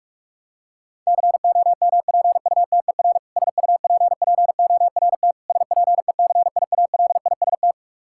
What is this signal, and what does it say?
Morse code, heard as a keyed tone, about 46 wpm, 700 Hz. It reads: COMPUTER SUPPORT SPECIALIST
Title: Practice QSO - Exchange 3 - More Info - 45wpm